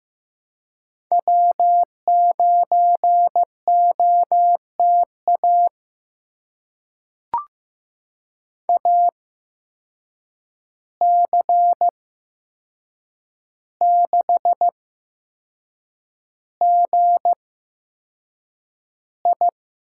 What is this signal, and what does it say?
Morse code, heard as a keyed tone, about 15 wpm, spaced as for 6 wpm, 700 Hz. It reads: W9OTA AC6GI
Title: USA Call Signs - 5 Characters - 8x Character Spacing - 15wpm